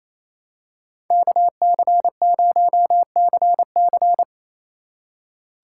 Morse code, heard as a keyed tone, about 28 wpm, 690 Hz. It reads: KC0CC
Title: USA Call Signs - 5 Characters - 4x Character Spacing - 28wpm